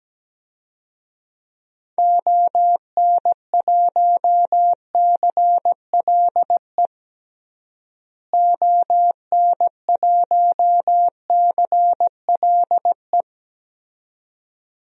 Morse code, heard as a keyed tone, about 17 wpm, 700 Hz. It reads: ON1CLE ON1CLE
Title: DXCC-100 Call Signs - 6 Characters - Repeated 2x - 17wpm